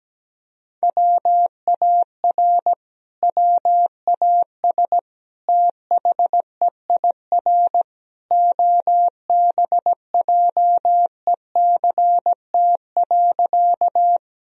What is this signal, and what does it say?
Morse code, heard as a keyed tone, about 17 wpm, 700 Hz. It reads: WAR WAS THEIR OBJECT.
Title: Sentences from Top 1000 Words - Review Part 3 - 17wpm